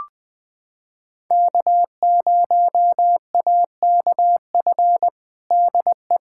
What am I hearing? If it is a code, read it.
Morse code, heard as a keyed tone, about 21 wpm, 700 Hz. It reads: K0AKF DE
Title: USA Callsign DE Callsign - 5 Characters - 20wpm